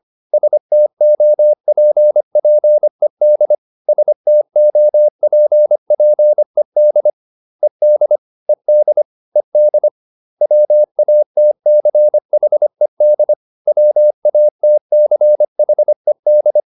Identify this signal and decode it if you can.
Morse code, heard as a keyed tone, about 25 wpm, 600 Hz. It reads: STOPPED STOPPED ED ED ED WATCHED WATCHED